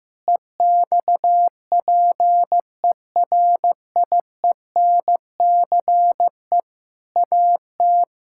Morse code, heard as a keyed tone, about 16 wpm, 700 Hz. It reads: EXPERIENCE AT
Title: Sentences from Top 1000 Words - Review Part 6 - 15wpm